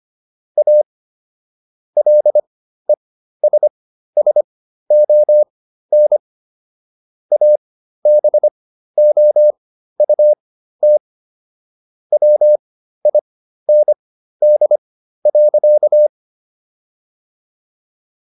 Morse code, heard as a keyed tone, about 25 wpm, spaced as for 13 wpm, 600 Hz. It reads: A LESSON ABOUT WIND.